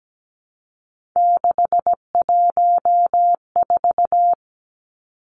Morse code, heard as a keyed tone, about 17 wpm, 700 Hz. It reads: 614
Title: Three Challenging Letter-Number Combinations 17wpm